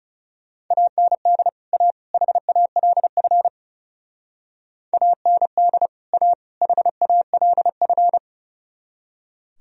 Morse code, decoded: AND A HALF AND A HALF

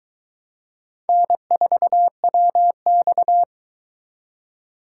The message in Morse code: N4WX

